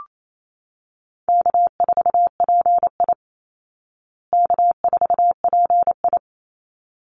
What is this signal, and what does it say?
Morse code, heard as a keyed tone, about 28 wpm, 700 Hz. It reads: K4PI K4PI
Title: Call Signs Repeated 3x - 28wpm